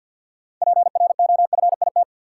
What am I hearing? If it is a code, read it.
Morse code, heard as a keyed tone, about 50 wpm, 700 Hz. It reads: PROFIT
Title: Sentences from Top 800 Words - New Word Drill - 50wpm